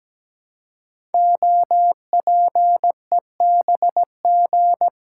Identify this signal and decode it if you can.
Morse code, heard as a keyed tone, about 17 wpm, 700 Hz. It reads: OPEBG
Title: Code Groups - 5 Letter:Numbers - 17wpm